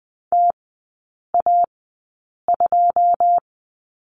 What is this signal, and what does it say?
Morse code, heard as a keyed tone, about 20 wpm, spaced as for 8 wpm, 690 Hz. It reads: TA2
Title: 081 Lesson 14e - Number 2 - Three Characters 20wpm